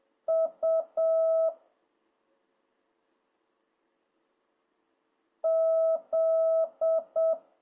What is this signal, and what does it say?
Morse code, heard as a keyed tone, about 7 wpm, 650 Hz. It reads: U Z